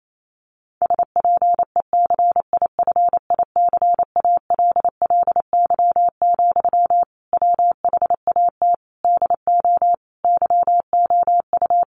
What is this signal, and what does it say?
Morse code, heard as a keyed tone, about 28 wpm, 700 Hz. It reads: SPECIFICALLY, WHAT DO YOU
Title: Sentences from Top 1100 Words 28wpm